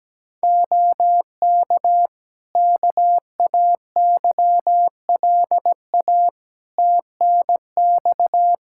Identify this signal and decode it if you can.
Morse code, heard as a keyed tone, about 17 wpm, 700 Hz. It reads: OK KAYLA TNX
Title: K1USN SST QSO Practice 17wpm